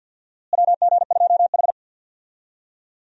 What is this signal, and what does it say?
Morse code, heard as a keyed tone, about 51 wpm, 700 Hz. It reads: WG2H
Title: USA Call Signs - 4 Characters - Repeated 2x - 50wpm